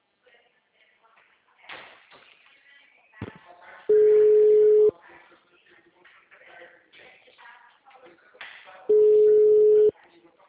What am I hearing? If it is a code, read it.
Ringback tone (Europe, 425 Hz), with speech.
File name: out- -2009- - - .18289